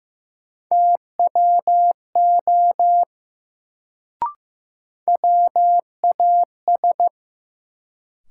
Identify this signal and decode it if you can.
Morse code, heard as a keyed tone, about 15 wpm, 700 Hz. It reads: TWO WAS
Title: Sentences from Top 100 Words - New Word Drill - 15wpm